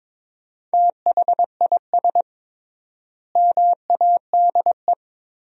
Morse code, heard as a keyed tone, about 23 wpm, 700 Hz. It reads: THIS MADE
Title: 3x Word-Spacing - Sentences from Top 500 Words - 22wpm